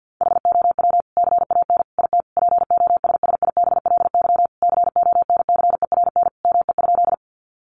Morse code, heard as a keyed tone, about 50 wpm, 700 Hz. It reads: HOW CAN IT POSSIBLY CONCERN ME?
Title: Sentences from Top 1300 Words 50wpm